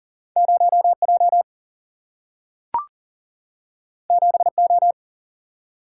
Morse code, heard as a keyed tone, about 39 wpm, 700 Hz. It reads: 0J 7O
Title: Single-Letter Single-Number 40wpm